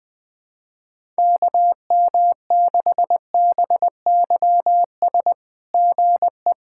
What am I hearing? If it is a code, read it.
Morse code, heard as a keyed tone, about 20 wpm, 690 Hz. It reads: KM6BYS GE
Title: SOTA QSO Practice 20wpm